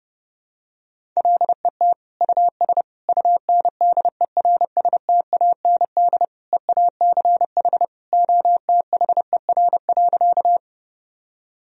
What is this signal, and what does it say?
Morse code, heard as a keyed tone, about 29 wpm, 700 Hz. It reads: LET US UNDERSTAND EACH OTHER.
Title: Sentences from Top 1000 Words - Review Part 1 - 30wpm